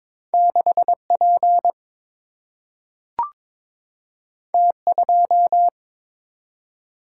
Morse code, heard as a keyed tone, about 23 wpm, 700 Hz. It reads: 6 P T 2
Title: Single-Letter Single-Number 22wpm